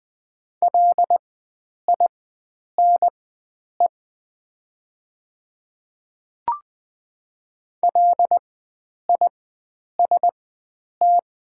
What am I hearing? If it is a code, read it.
Morse code, heard as a keyed tone, about 20 wpm, spaced as for 9 wpm, 700 Hz. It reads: LINE LIST
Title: 4x Character Spacing - Top 500 Words - 20wpm